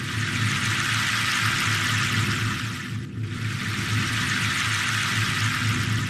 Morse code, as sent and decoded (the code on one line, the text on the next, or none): none